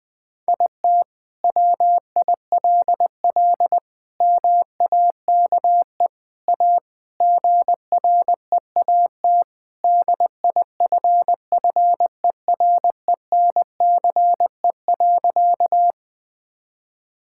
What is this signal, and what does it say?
Morse code, heard as a keyed tone, about 21 wpm, 700 Hz. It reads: IT WILL MAKE A GREAT DIFFERENCE.
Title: Sentences from Top 1000 Words - Review Part 5 - 20wpm